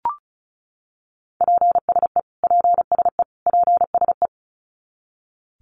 Morse code, heard as a keyed tone, about 34 wpm, 700 Hz. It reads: PSE PSE PSE
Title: Abbreviations Repeated 3x - 35wpm